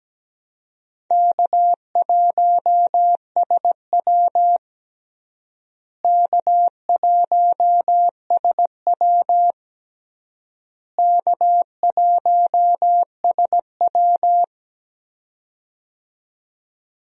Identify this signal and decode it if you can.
Morse code, heard as a keyed tone, about 17 wpm, 700 Hz. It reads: K1SW K1SW K1SW
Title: USA Call Signs - 4 Characters - Repeated 3x - 17wpm